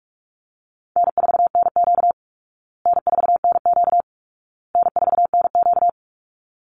Morse code, heard as a keyed tone, about 44 wpm, 700 Hz. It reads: N4NQ N4NQ N4NQ
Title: Call Signs Repeated 2x - 45wpm